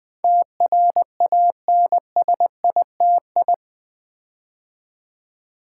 Morse code, heard as a keyed tone, about 20 wpm, 700 Hz. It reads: TRANSITI
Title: Build A Word - 4 Letters At A Time - 10-Letter Words - 20wpm